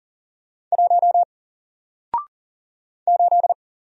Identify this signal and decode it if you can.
Morse code, heard as a keyed tone, about 40 wpm, 700 Hz. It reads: 1 8
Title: Single Letter-Number 40wpm